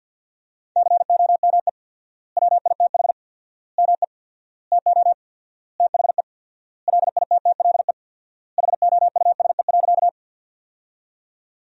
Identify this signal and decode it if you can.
Morse code, heard as a keyed tone, about 50 wpm, 700 Hz. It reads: COME WITH ME TO THE LITTLE HOUSE.
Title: 4x Word-Spacing - Sentences from Top 200 Words - 50wpm